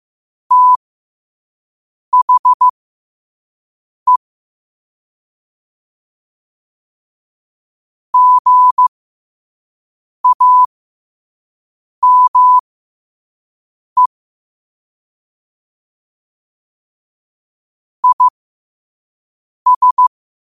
Morse code, heard as a keyed tone, about 15 wpm, spaced as for 5 wpm, 1000 Hz. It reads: THE GAME IS
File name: qotdfiles / 5wpm